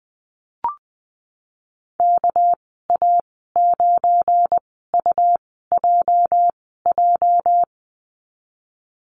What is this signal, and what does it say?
Morse code, heard as a keyed tone, about 20 wpm, spaced as for 15 wpm, 700 Hz. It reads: KA9UJJ
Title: 213 Lesson 34g - Number 8 - Call Signs 20wpm